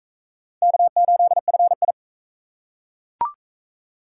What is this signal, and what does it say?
Morse code, heard as a keyed tone, about 35 wpm, 700 Hz. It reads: K8FI